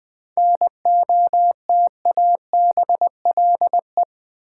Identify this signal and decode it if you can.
Morse code, heard as a keyed tone, about 20 wpm, 700 Hz. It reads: NOTABLE
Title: 7 Letter Words - First 3 Letters Given - 20wpm